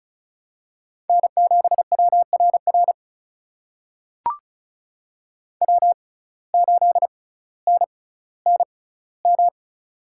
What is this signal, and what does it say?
Morse code, heard as a keyed tone, about 34 wpm, spaced as for 18 wpm, 700 Hz. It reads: N7WRR W8NNM
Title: USA Call Signs - 5 Characters - 6x Character Spacing - 35wpm